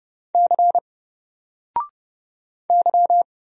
Morse code, heard as a keyed tone, about 31 wpm, 700 Hz. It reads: C Y
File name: Single-Letter-Number-30wpm